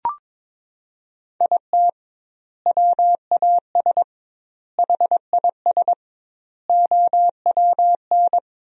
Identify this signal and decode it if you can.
Morse code, heard as a keyed tone, about 21 wpm, 700 Hz. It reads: IT WAS HIS OWN
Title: 2x Word-Spacing - Sentences from Top 300 Words - 22wpm